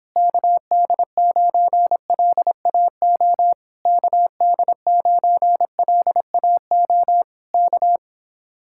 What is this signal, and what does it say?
Morse code, heard as a keyed tone, about 26 wpm, 700 Hz. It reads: KD9LAO KD9LAO K